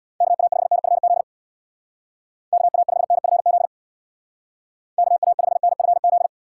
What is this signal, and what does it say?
Morse code, heard as a keyed tone, about 65 wpm, 700 Hz. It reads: KA5NFZ KA5NFZ KA5NFZ